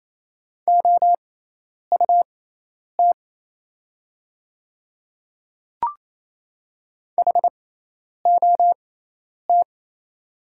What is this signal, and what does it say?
Morse code, heard as a keyed tone, about 28 wpm, spaced as for 10 wpm, 690 Hz. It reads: OUT HOT
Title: 6x Character Spacing - Top 100 Words - 28wpm